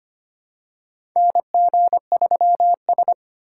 Morse code, heard as a keyed tone, about 25 wpm, 700 Hz. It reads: NG3S
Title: USA Call Signs - 4 Characters - 4x Character Spacing - 25wpm